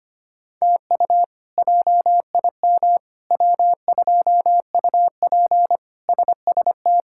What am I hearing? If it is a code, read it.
Morse code, heard as a keyed tone, about 25 wpm, 700 Hz. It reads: TU JIM W2UP SST
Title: K1USN SST QSO Practice - 25wpm